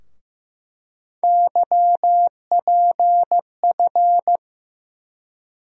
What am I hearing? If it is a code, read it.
Morse code, heard as a keyed tone, about 15 wpm, 700 Hz. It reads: YPF
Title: USA Call Sign Postfixes - 3 Letters 15wpm